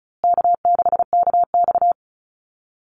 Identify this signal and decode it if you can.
Morse code, heard as a keyed tone, about 34 wpm, 700 Hz. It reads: K6KX
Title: Easy Call Signs Repeated 3x - 35wpm